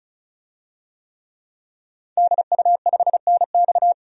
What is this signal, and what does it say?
Morse code, heard as a keyed tone, about 35 wpm, 700 Hz. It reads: DU5NX
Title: DXCC-100 Call Signs - 5 Characters - 35wpm